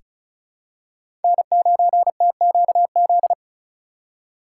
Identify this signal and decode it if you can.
Morse code, heard as a keyed tone, about 36 wpm, 700 Hz. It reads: N9TQZ
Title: USA Call Signs - 5 Characters - 4x Character Spacing - 35wpm